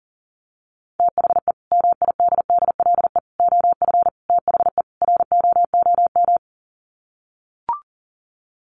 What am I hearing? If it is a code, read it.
Morse code, heard as a keyed tone, about 40 wpm, 700 Hz. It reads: THE MIDDLE OF THE ROOM
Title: Sets of 5 Words Encore2 40wpm